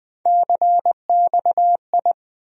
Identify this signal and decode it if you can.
Morse code, heard as a keyed tone, about 20 wpm, 700 Hz. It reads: CXI
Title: USA Call Sign Postfixes - 3 Letters 20wpm